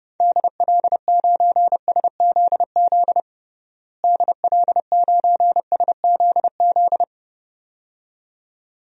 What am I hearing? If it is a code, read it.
Morse code, heard as a keyed tone, about 30 wpm, 700 Hz. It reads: DL9SZZ DL9SZZ
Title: DXCC-100 Call Signs - 6 Characters - Repeated 2x - 30wpm